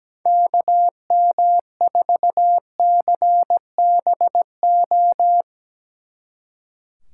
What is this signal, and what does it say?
Morse code, heard as a keyed tone, about 18 wpm, 700 Hz. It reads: KM4CBO
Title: Speed Racing - USA 2x3 Call Signs 17wpm